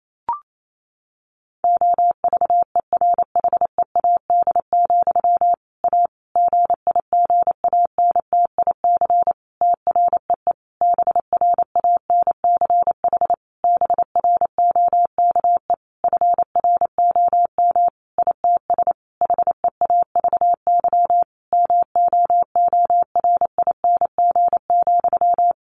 Morse code, heard as a keyed tone, about 28 wpm, 700 Hz. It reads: OVERHEAD, A GIGANTIC TREE BRANCH BROKE FROM ITS HEAVY MOORING,
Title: A Sound Of Thunder 28wpm